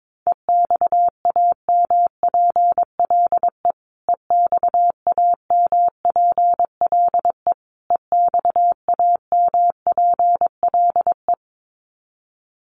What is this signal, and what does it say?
Morse code, heard as a keyed tone, about 22 wpm, 700 Hz. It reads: EXAMPLE EXAMPLE EXAMPLE
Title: Top 300 Words Repeated 22wpm